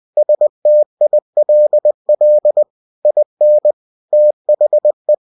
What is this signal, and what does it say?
Morse code, heard as a keyed tone, about 20 wpm, 600 Hz. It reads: STILL IN THE
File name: cw-books / winnie-the-pooh / winnie-20wpm-page-0053